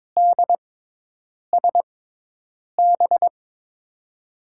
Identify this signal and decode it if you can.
Morse code, heard as a keyed tone, about 22 wpm, spaced as for 8 wpm, 700 Hz. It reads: DSB
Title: USA Call Signs - 5 Characters - 6x Character Spacing - 22wpm